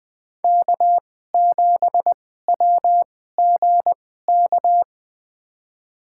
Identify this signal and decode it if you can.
Morse code, heard as a keyed tone, about 20 wpm, spaced as for 14 wpm, 700 Hz. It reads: K7WGK
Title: USA Call Signs - 2x Character Spacing - 20wpm